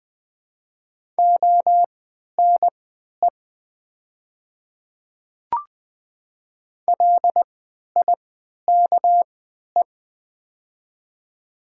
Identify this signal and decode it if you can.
Morse code, heard as a keyed tone, about 20 wpm, spaced as for 11 wpm, 700 Hz. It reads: ONE LIKE
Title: 3x Character Spacing - Top 200 Words - 20wpm